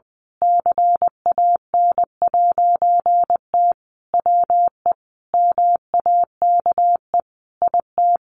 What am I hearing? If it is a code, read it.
Morse code, heard as a keyed tone, about 20 wpm, 700 Hz. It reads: CAN'T WE MAKE IT
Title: A Sound Of Thunder 20wpm